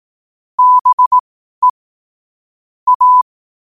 Morse code, heard as a keyed tone, about 18 wpm, spaced as for 12 wpm, 1000 Hz. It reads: BE A
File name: qotdfiles / 12wpm